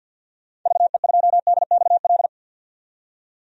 Morse code, heard as a keyed tone, about 51 wpm, 700 Hz. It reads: VE2DXL
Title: DXCC-100 Call Signs - 6 Characters - 50wpm